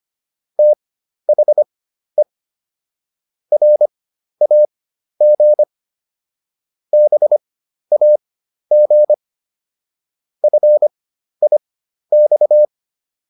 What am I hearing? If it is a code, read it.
Morse code, heard as a keyed tone, about 25 wpm, spaced as for 12 wpm, 600 Hz. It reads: THE RAG BAG FIX